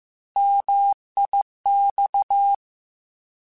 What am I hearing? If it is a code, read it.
Morse code, heard as a keyed tone, about 15 wpm, 800 Hz. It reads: MIX
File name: Contrasts / Contrasts-mp3s / 3LtrContrasts @13